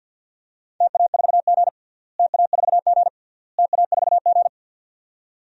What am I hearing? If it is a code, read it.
Morse code, heard as a keyed tone, about 53 wpm, 700 Hz. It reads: TA4G TA4G TA4G
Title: DXCC-100 Call Signs - 4 Characters - Repeated 3x - 50wpm